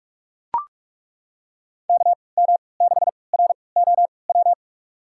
Morse code, heard as a keyed tone, about 42 wpm, spaced as for 25 wpm, 700 Hz. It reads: KM6ROW
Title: USA Call Signs - 6 Characters - 3x Character Spacing - 45wpm